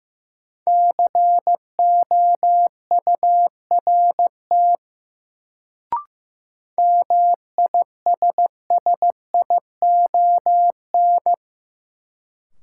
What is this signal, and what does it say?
Morse code, heard as a keyed tone, about 16 wpm, 700 Hz. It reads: COURT MISSION
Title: Sentences from Top 1600 Words - New Word Drill Alt - 15wpm